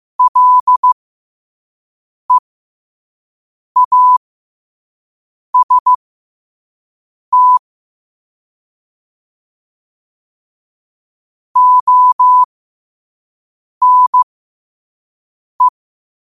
Morse code, heard as a keyed tone, about 15 wpm, spaced as for 5 wpm, 1000 Hz. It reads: LEAST ONE
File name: qotdfiles / 5wpm